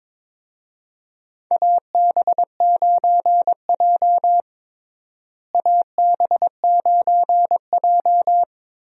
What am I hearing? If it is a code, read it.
Morse code, heard as a keyed tone, about 22 wpm, 700 Hz. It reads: AB9J AB9J
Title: USA Call Signs - 4 Characters - Repeated 2x - 22wpm